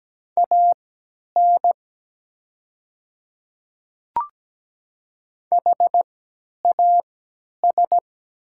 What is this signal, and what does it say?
Morse code, heard as a keyed tone, about 17 wpm, spaced as for 10 wpm, 700 Hz. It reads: AN HAS
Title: 3x Character Spacing - Top 100 Words - 17wpm